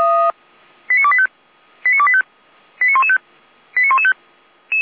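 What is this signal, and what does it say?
Five-tone selective call (ZVEI), heard as quick runs of five tones, tones of about 70 ms. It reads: 89285 89285 89115 89115
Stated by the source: ZVEI five-tone selective call received on an SDR.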